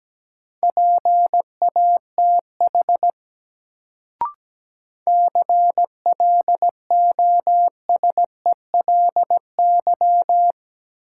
Morse code, heard as a keyed tone, about 17 wpm, 700 Hz. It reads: PATH CLOSELY
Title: Sentences from Top 1400 Words - New Word Drill - 17wpm